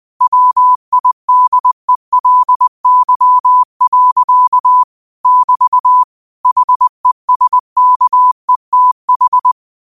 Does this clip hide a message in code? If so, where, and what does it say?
Yes: Morse code from 0.2 to 9.5 s, heard as a keyed tone, about 20 wpm, 1000 Hz. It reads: WIDELY. <BT> HESKETH